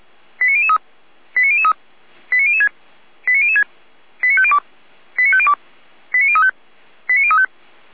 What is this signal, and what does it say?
Five-tone selective call (ZVEI), heard as quick runs of five tones, tones of about 70 ms. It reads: 89003 89003 89006 89006 89502 89502 89035 89035